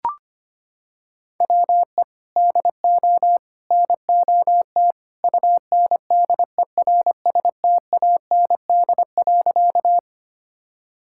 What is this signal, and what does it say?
Morse code, heard as a keyed tone, about 25 wpm, 700 Hz. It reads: WE DO NOT UNDERSTAND.
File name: practice / 25wpm / Sentences with 3-Word Phrases - 25wpm